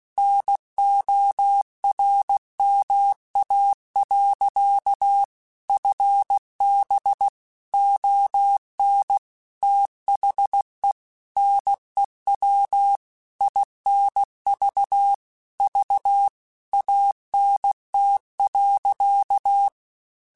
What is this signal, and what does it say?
Morse code, heard as a keyed tone, about 16 wpm, 800 Hz. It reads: NORMA. FB ON THE NEW INV V ANT.
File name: QSOs / QSOs-mp3s / WD0FBT 1@15